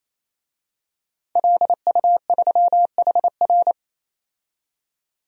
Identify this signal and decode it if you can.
Morse code, heard as a keyed tone, about 28 wpm, 700 Hz. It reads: LU3HR
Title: DXCC-100 Call Signs - 5 Characters - 28wpm